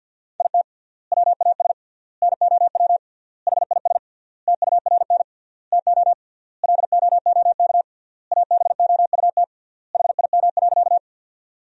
Morse code, heard as a keyed tone, about 46 wpm, 700 Hz. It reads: IT WAS NOW HIS TURN TO LOOK ABOUT HIM.
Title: 3x Word-Spacing - Sentences from Top 200 Words - 50wpm